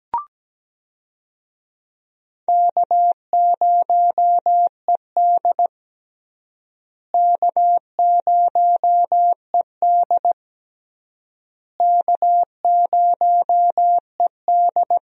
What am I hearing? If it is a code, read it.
Morse code, heard as a keyed tone, about 17 wpm, 700 Hz. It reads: K0ED K0ED K0ED
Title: USA Call Signs - 4 Characters - Repeated 3x - 17wpm